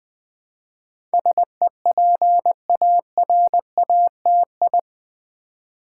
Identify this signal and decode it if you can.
Morse code, heard as a keyed tone, about 20 wpm, 700 Hz. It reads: SEPARATI
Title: Build A Word - 4 Letters At A Time - 10-Letter Words - 20wpm